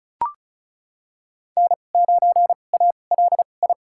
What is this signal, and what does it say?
Morse code, heard as a keyed tone, about 34 wpm, spaced as for 25 wpm, 700 Hz. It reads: N9ALI
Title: USA Call Signs - 5 Characters - 2x Character Spacing - 35wpm